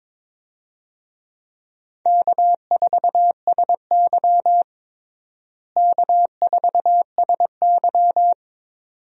Morse code, heard as a keyed tone, about 22 wpm, 700 Hz. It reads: K4SY K4SY
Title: USA Call Signs - 4 Characters - Repeated 2x - 22wpm